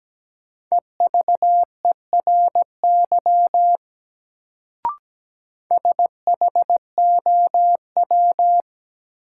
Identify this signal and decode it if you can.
Morse code, heard as a keyed tone, about 17 wpm, 700 Hz. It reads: EVERY SHOW
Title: Sets of 5 Words - New Word Drill - 17wpm